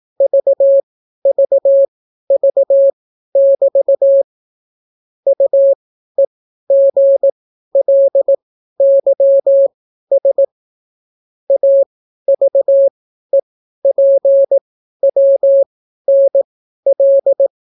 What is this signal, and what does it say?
Morse code, heard as a keyed tone, about 18 wpm, spaced as for 12 wpm, 550 Hz. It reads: VVV<BT> UEGLYS AVEPWNL